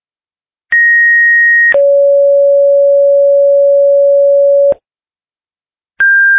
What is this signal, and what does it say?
Two-tone page, heard as a one-second tone, then a three-second tone, about 1900 then 560 Hz. It is heard from 0.7 to 4.7 s.